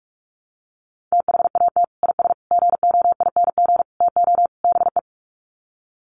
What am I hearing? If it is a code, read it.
Morse code, heard as a keyed tone, about 44 wpm, 700 Hz. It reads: THAT IS GOING TO BE
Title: Sets of 5 Words Encore2 45wpm